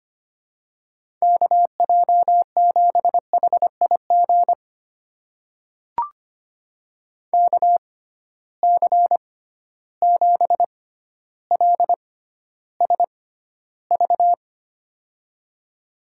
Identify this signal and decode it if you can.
Morse code, heard as a keyed tone, about 25 wpm, spaced as for 13 wpm, 700 Hz. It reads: KJ7HIG KC7LSV
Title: USA Call Signs - 6 Characters - 6x Character Spacing - 25wpm